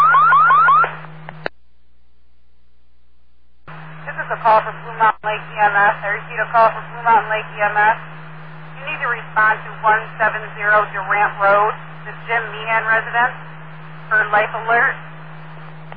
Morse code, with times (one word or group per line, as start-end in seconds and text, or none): none